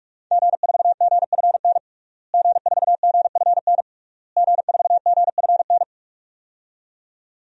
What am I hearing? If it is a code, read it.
Morse code, heard as a keyed tone, about 46 wpm, 700 Hz. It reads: G4GFN G4GFN G4GFN
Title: DXCC-100 Call Signs - 5 Characters - Repeated 3x - 45wpm